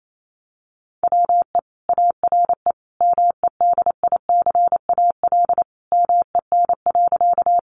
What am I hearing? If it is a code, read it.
Morse code, heard as a keyed tone, about 28 wpm, 700 Hz. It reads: WE ARE MEDICAL MEN.
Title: Sentences from Top 800 Words 28wpm